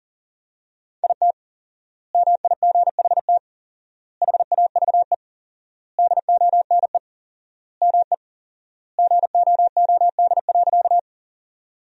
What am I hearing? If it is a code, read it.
Morse code, heard as a keyed tone, about 40 wpm, 700 Hz. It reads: IT MIGHT HAVE DONE ME GOOD.
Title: 4x Word-Spacing - Sentences from Top 500 Words - 40wpm